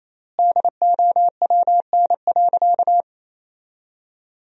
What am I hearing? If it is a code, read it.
Morse code, heard as a keyed tone, about 28 wpm, 700 Hz. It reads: DOWN.
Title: 4x Word-Spacing - Sentences from Top 300 Words - 28wpm